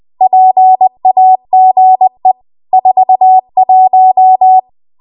Morse code, heard as a keyed tone, about 20 wpm, 750 Hz. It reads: PAGE 41
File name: Archive / 20 WPM / 20WPM